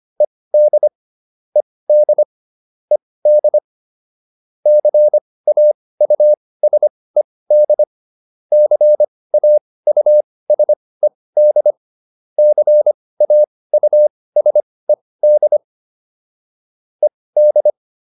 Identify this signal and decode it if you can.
Morse code, heard as a keyed tone, about 25 wpm, spaced as for 18 wpm, 600 Hz. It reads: ED ED ED CAUSED CAUSED CAUSED ED